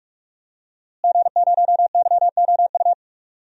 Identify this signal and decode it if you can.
Morse code, heard as a keyed tone, about 44 wpm, 700 Hz. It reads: G0YOU
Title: DXCC-100 Call Signs - 5 Characters - Repeated 2x - 45wpm